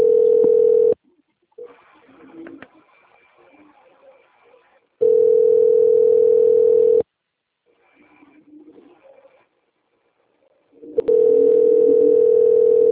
Ringback tone (North America), 440 + 480 Hz.